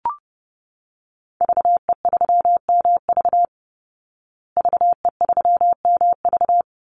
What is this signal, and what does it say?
Morse code, heard as a keyed tone, about 30 wpm, 700 Hz. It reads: VE3MV VE3MV
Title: Call Signs Repeated 3x - 30wpm